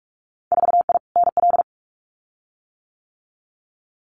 Morse code, heard as a keyed tone, about 46 wpm, 700 Hz. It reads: 4I NL
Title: Winter Field Day Exchanges Encore 45wpm